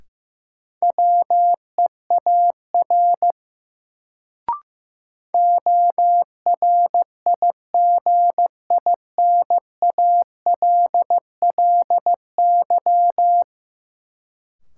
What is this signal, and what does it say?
Morse code, heard as a keyed tone, about 15 wpm, 700 Hz. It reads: WEAR ORIGINALLY
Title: Sentences from Top 1500 Words - New Word Drill Alt - 15wpm